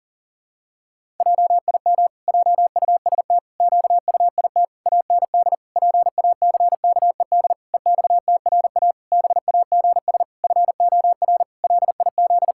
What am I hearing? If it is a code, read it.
Morse code, heard as a keyed tone, about 37 wpm, 700 Hz. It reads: JIM JUST QUIT AND PACKED EXTRA BAGS FOR LIZ
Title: Pangram Challenge v2 - 40wpm